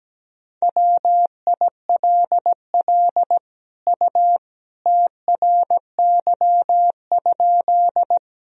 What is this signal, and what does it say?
Morse code, heard as a keyed tone, about 17 wpm, 700 Hz. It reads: WILL U TRY?
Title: Sentences from Top 200 Words and Abbreviations 17wpm